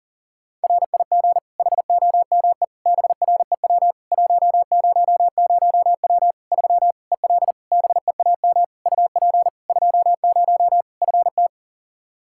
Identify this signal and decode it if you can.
Morse code, heard as a keyed tone, about 40 wpm, 700 Hz. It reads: RIG HOME BREW 100W 3 EL BEAM UP 20 FT
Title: Practice QSO - Exchange 2 - Info - 40wpm